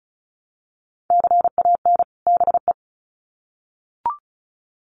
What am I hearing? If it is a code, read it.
Morse code, heard as a keyed tone, about 34 wpm, 700 Hz. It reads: CAN BE
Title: Sets of 2 Words 35wpm